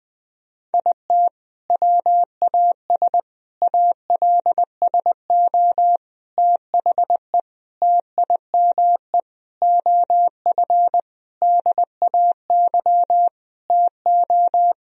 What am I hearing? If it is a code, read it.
Morse code, heard as a keyed tone, about 20 wpm, 700 Hz. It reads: IT WAS ALSO THE TIME OF DAY TO
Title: Sentences from Top 1000 Words - Review Part 1 - 20wpm